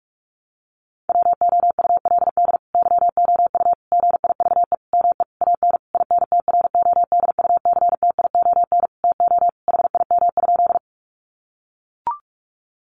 Morse code, heard as a keyed tone, about 44 wpm, 700 Hz. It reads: WOULD YOU GIVE ME AN INTRODUCTION TO HIM?
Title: Sentences from Top 1000 Words 45wpm